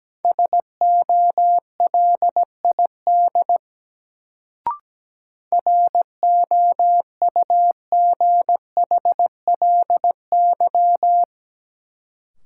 Morse code, heard as a keyed tone, about 17 wpm, 700 Hz. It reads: SOLID ROUGHLY
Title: Sentences from Top 1500 Words - New Word Drill Alt - 17wpm